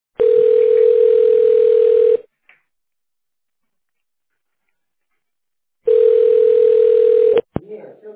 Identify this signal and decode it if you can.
Ringback tone (North America), 440 + 480 Hz, with speech.
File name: 1040.3000.4ead03b8-f47c-43b6-a1bd